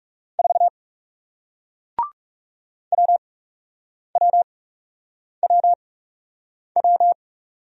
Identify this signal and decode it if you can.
Morse code, heard as a keyed tone, about 41 wpm, spaced as for 8 wpm, 700 Hz. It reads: 4 WWWW